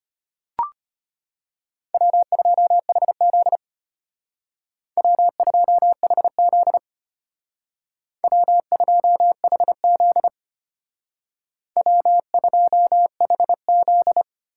Morse code, heard as a keyed tone, about 33 wpm, 700 Hz. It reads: W2HZ W2HZ W2HZ W2HZ